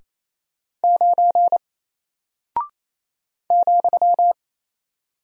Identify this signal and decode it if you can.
Morse code, heard as a keyed tone, about 28 wpm, 700 Hz. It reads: 9 ,
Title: Single Character (Letter:Number:Punctuation:Prosign) 28wpm